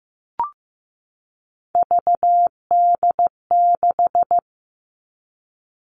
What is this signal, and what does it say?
Morse code, heard as a keyed tone, about 15 wpm, 700 Hz. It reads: VD6
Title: Three Challenging Letter-Number Combinations 15wpm